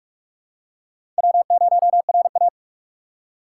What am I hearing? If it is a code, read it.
Morse code, heard as a keyed tone, about 42 wpm, 700 Hz. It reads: W0RA